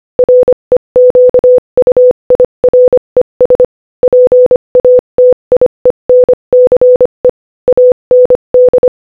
Morse code, heard as a keyed tone, about 25 wpm, 500 Hz. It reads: REQUIRES PATIENCE AND
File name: Morse-Code-Audio-Speed-25-7